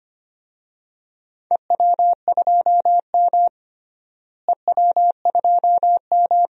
Morse code, heard as a keyed tone, about 25 wpm, 700 Hz. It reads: EW2M EW2M
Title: DXCC-100 Call Signs - 4 Characters - Repeated 2x - 25wpm